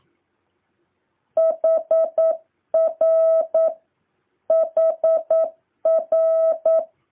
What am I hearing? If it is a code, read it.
Morse code, heard as a keyed tone, about 9 wpm, 650 Hz. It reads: HR HR
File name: jh8cbh / 2023cw